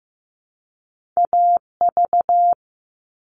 Morse code, heard as a keyed tone, about 15 wpm, 700 Hz. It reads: AV
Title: All Cut Numbers - 2-Digit 15wpm